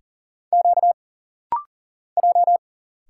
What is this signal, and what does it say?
Morse code, heard as a keyed tone, about 40 wpm, 700 Hz. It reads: Q J
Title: Single Letter-Number - Rapid-Fire 40wpm